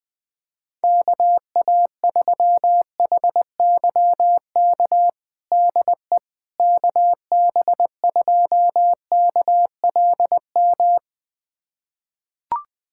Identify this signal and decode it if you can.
Morse code, heard as a keyed tone, about 20 wpm, 700 Hz. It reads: KA3HYK DE KB2KLM
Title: USA Callsign DE Callsign - 6 Characters - 20wpm